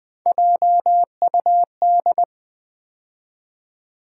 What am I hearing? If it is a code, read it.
Morse code, heard as a keyed tone, about 19 wpm, 700 Hz. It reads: JUD
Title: Build A Word - 3 Letters At A Time - 9-Letter Words - 20wpm